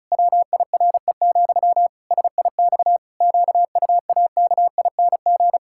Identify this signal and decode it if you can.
Morse code, heard as a keyed tone, about 36 wpm, 700 Hz. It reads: WIRE, SIX QUAKING
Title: Pangram Challenge v2 - 35wpm